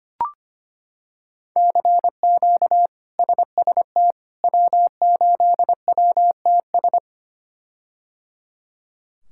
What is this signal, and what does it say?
Morse code, heard as a keyed tone, about 25 wpm, 700 Hz. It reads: CQ SST W8WTS
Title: K1USN SST QSO Practice - 25wpm